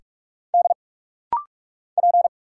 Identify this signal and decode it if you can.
Morse code, heard as a keyed tone, about 46 wpm, 700 Hz. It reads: D P
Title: Single Letter-Number - Rapid-Fire 45wpm